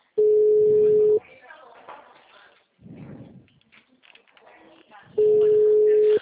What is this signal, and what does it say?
Ringback tone (Europe, 425 Hz), with speech.